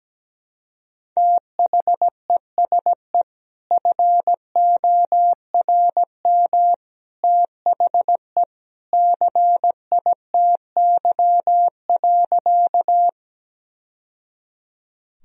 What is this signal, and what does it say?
Morse code, heard as a keyed tone, about 17 wpm, 700 Hz. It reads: THESE FORM THE CITY.
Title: Sentences from Top 1000 Words - Review Part 1 - 17wpm